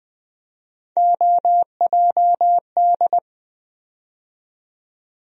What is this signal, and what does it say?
Morse code, heard as a keyed tone, about 20 wpm, 700 Hz. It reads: OJD
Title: USA Call Sign Postfixes - 3 Letters 20wpm